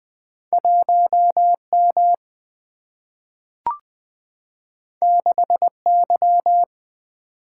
Morse code, heard as a keyed tone, about 20 wpm, 700 Hz. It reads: 1 M 6 Y
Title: Single-Letter Single-Number 20wpm